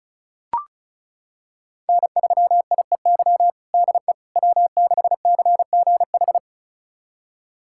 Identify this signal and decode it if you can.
Morse code, heard as a keyed tone, about 36 wpm, 700 Hz. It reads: N3IEY DE W6CGH
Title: USA Callsign DE Callsign - 5 Characters - 35wpm